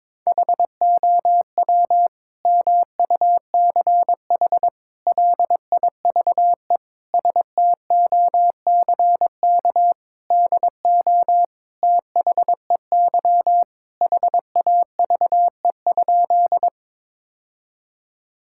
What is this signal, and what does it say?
Morse code, heard as a keyed tone, about 22 wpm, 700 Hz. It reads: HOW MUCH LIVE STOCK DO THEY HAVE?
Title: Sentences from Top 1000 Words - Review Part 5 - 22wpm